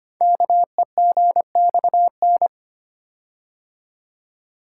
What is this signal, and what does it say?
Morse code, heard as a keyed tone, about 25 wpm, 700 Hz. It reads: KEGXN
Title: Code Groups - 5 Letters - 25wpm